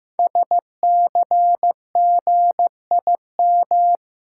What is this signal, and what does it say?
Morse code, heard as a keyed tone, about 15 wpm, 700 Hz. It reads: SCGIM
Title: Code Groups - 5 Letter:Numbers - 15wpm